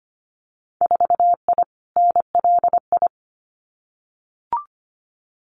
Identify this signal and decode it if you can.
Morse code, heard as a keyed tone, about 25 wpm, 700 Hz. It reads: 4I NLI
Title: Winter Field Day Exchanges Encore 25wpm